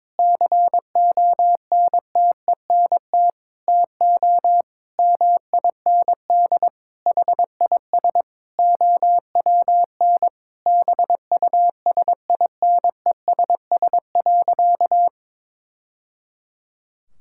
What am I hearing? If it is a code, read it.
Morse code, heard as a keyed tone, about 22 wpm, 700 Hz. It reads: CONTENT TO MIND HIS OWN BUSINESS.
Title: Sentences from Top 1000 Words - Review Part 6 - 22wpm